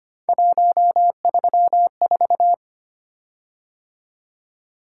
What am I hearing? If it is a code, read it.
Morse code, heard as a keyed tone, about 25 wpm, 700 Hz. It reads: 134
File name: Three Letter-Number Combinations - 25wpm